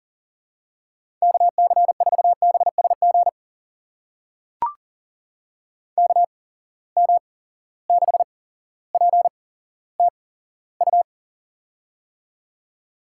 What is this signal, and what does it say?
Morse code, heard as a keyed tone, about 40 wpm, spaced as for 17 wpm, 700 Hz. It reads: KC4BSG KM6PTU
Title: USA Call Signs - 6 Characters - 8x Character Spacing - 40wpm